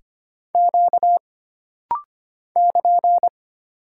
Morse code, heard as a keyed tone, about 25 wpm, 700 Hz. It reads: Q <KN>
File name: Single Character (Letter:Number:Punctuation:Prosign) - Rapid-Fire - 25wpm